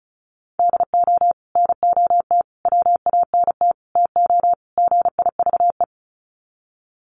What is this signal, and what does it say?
Morse code, heard as a keyed tone, about 34 wpm, 700 Hz. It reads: DO NOT WANT TO GIVE
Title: Sets of 5 Words Encore 35wpm